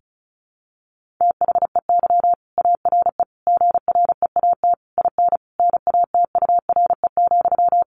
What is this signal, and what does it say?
Morse code, heard as a keyed tone, about 34 wpm, 700 Hz. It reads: THEY ARE GREAT IN NATURE,
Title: Sentences from Top 1400 Words 35wpm